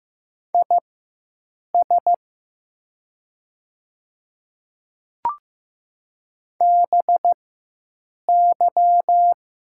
Morse code, heard as a keyed tone, about 15 wpm, spaced as for 7 wpm, 700 Hz. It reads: IS BY
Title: 4x Character Spacing - Top 100 Words - 15wpm